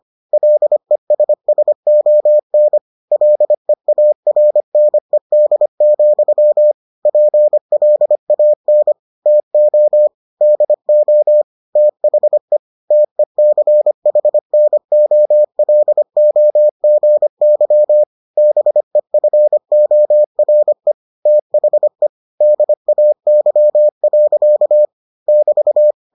Morse code, heard as a keyed tone, about 25 wpm, 600 Hz. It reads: LESSON LEARNED, PLAN TO DO THE TECHNOLOGY BEFORE THE DAY. <BT>